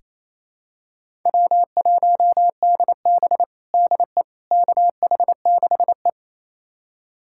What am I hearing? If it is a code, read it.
Morse code, heard as a keyed tone, about 28 wpm, 700 Hz. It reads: W1DB DE KH6E
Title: USA Callsign DE Callsign - 4 Characters - 28wpm